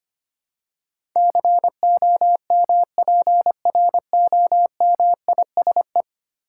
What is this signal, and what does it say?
Morse code, heard as a keyed tone, about 25 wpm, 700 Hz. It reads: COMPROMISE
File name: Ten Letter Words - 25wpm